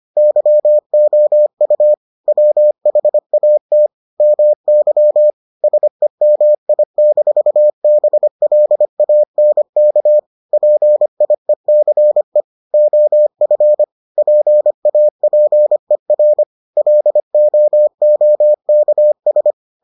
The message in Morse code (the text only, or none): YOU WHAT MY SEMI-BLANK PIECE OF PAPER LOOKS